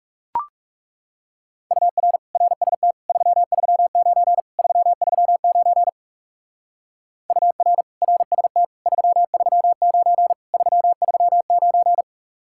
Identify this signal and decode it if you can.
Morse code, heard as a keyed tone, about 36 wpm, 700 Hz. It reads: UR RST 339 339 UR RST 339 339
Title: Speed-Racing Common QSO Phrases Encore - 30wpm